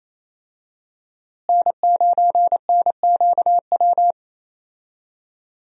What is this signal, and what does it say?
Morse code, heard as a keyed tone, about 27 wpm, 700 Hz. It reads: N9NQW
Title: USA Call Signs - 5 Characters - Repeated 3x - 28wpm